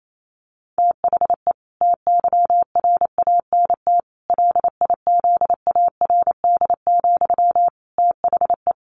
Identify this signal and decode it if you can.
Morse code, heard as a keyed tone, about 28 wpm, 700 Hz. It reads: THE TYRANT LIZARD, THE
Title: A Sound Of Thunder 28wpm